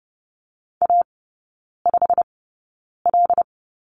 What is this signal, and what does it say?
Morse code, heard as a keyed tone, about 30 wpm, spaced as for 9 wpm, 700 Hz. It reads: A5L
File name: course / 30:10wpm / 087 Lesson 15e - Number 5 - Three Characters 30wpm